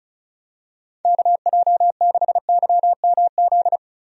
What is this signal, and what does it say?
Morse code, heard as a keyed tone, about 36 wpm, 700 Hz. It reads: KJ6YMZ
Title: USA Call Signs - 6 Characters - 35wpm